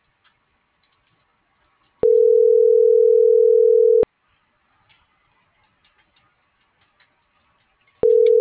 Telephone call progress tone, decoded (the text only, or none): ringback tone